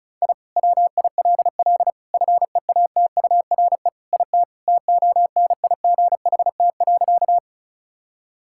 Morse code, heard as a keyed tone, about 34 wpm, 700 Hz. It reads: I WILL FEATURE IT TONIGHT.
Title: Sentences from Top 1000 Words - Review Part 5 - 35wpm